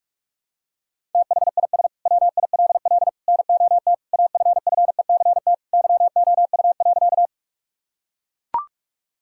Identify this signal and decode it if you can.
Morse code, heard as a keyed tone, about 43 wpm, 700 Hz. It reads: THIS WILL NOT AFFECT YOU.
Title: Sentences from Top 1000 Words - Review Part 6 - 45wpm